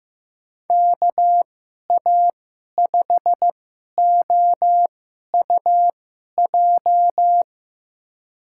Morse code, heard as a keyed tone, about 15 wpm, spaced as for 11 wpm, 700 Hz. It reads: KA5OUJ
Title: USA Call Signs - 6 Characters - 2x Character Spacing - 15wpm